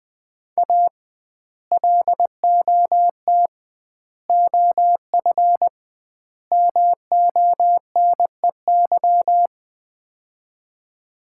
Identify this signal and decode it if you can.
Morse code, heard as a keyed tone, about 20 wpm, 700 Hz. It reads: A LOT OF MONEY
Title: 2x Word-Spacing - Sets of 4 Words - 20wpm